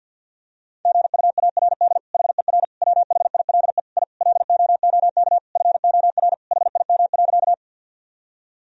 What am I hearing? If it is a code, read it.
Morse code, heard as a keyed tone, about 50 wpm, 700 Hz. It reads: GUARD HER WHILE I LOOK FOR HIM.
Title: Sentences from Top 1000 Words - Review Part 4 - 50wpm